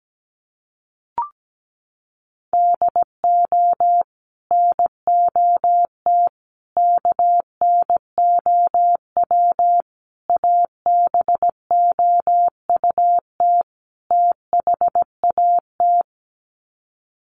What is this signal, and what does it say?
Morse code, heard as a keyed tone, about 17 wpm, 700 Hz. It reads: DO NOT KNOW ABOUT THAT
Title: Sets of 5 Words Encore2 17wpm